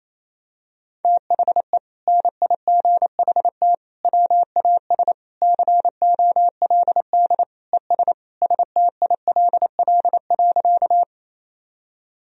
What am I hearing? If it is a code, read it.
Morse code, heard as a keyed tone, about 28 wpm, 700 Hz. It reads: THE NIGHT WAS COLD ES STILL.
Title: Sentences from Top 400 Words and Abbreviations 28wpm